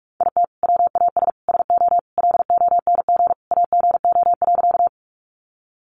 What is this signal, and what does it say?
Morse code, heard as a keyed tone, about 44 wpm, 700 Hz. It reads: IT WAS SO LONG AGO.
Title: Sentences from Top 500 Words 45wpm